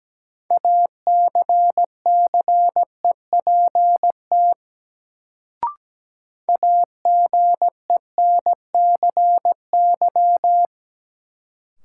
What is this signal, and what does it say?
Morse code, heard as a keyed tone, about 17 wpm, 700 Hz. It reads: ACCEPT AGENCY
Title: Sentences from Top 1300 Words - New Word Drill Alt - 17wpm